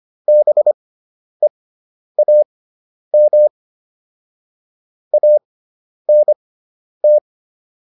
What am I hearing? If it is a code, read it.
Morse code, heard as a keyed tone, about 25 wpm, spaced as for 10 wpm, 600 Hz. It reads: BEAM ANT